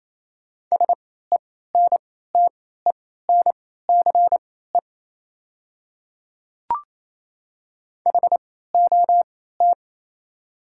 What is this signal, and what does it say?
Morse code, heard as a keyed tone, about 28 wpm, spaced as for 16 wpm, 700 Hz. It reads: SENTENCE HOT
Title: 3x Character Spacing - Top 200 Words - 28wpm